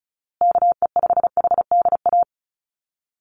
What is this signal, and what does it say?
Morse code, heard as a keyed tone, about 34 wpm, 700 Hz. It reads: KE5HDA
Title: USA Call Signs Repeated 2x 35wpm